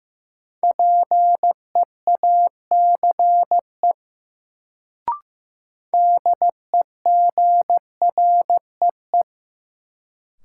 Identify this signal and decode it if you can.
Morse code, heard as a keyed tone, about 15 wpm, 700 Hz. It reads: PEACE DEGREE
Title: Sentences from Top 1500 Words - New Word Drill - 15wpm